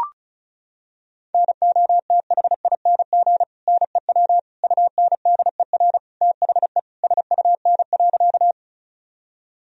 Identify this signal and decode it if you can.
Morse code, heard as a keyed tone, about 36 wpm, 700 Hz. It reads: NOTHING NEW UNDER THE SUN.
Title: Sentences from Top 1000 Words - Review Part 1 - 35wpm